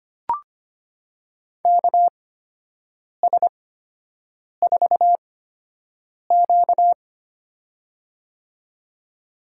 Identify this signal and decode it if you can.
Morse code, heard as a keyed tone, about 25 wpm, spaced as for 7 wpm, 700 Hz. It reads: KS4Q
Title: USA Call Signs - 4 Characters - 8x Character Spacing - 25wpm